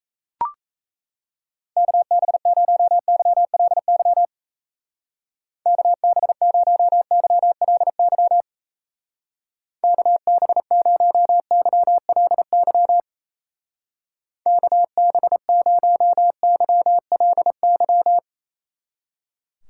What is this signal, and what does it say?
Morse code, heard as a keyed tone, about 37 wpm, 700 Hz. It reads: KB0YLY KB0YLY KB0YLY KB0YLY